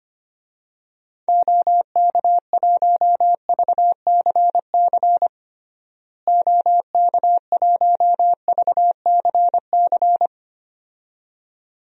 Morse code, heard as a keyed tone, about 25 wpm, 700 Hz. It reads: OK1VCC OK1VCC
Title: DXCC-100 Call Signs - 6 Characters - Repeated 2x - 25wpm